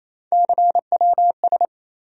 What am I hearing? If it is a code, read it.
Morse code, heard as a keyed tone, about 27 wpm, 700 Hz. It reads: CWS
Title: USA Call Sign Postfixes - 3 Letters 28wpm